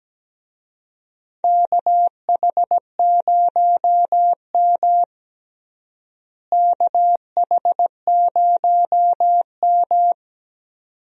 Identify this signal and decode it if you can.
Morse code, heard as a keyed tone, about 17 wpm, 700 Hz. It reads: KH0M KH0M
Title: USA Call Signs - 4 Characters - Repeated 2x - 17wpm